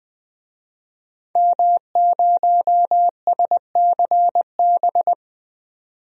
Morse code, heard as a keyed tone, about 20 wpm, 700 Hz. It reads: M0SCB
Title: DXCC-100 Call Signs - 5 Characters - 20wpm